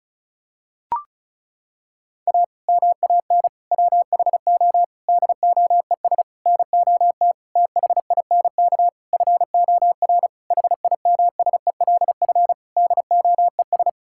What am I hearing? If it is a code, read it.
Morse code, heard as a keyed tone, about 36 wpm, 700 Hz. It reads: A MAN WHO DOES NOT THINK FOR HIMSELF DOES
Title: Sentences from Top 1600 Words 35wpm